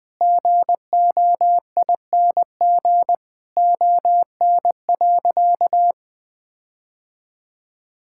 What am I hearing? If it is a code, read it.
Morse code, heard as a keyed tone, about 20 wpm, 700 Hz. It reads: GOING ON.
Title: Sentences with 4-Word Phrases - Spoken Beginning - 20wpm